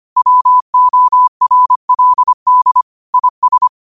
Morse code, heard as a keyed tone, about 25 wpm, 1000 Hz. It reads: WORLD IS
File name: 25wpm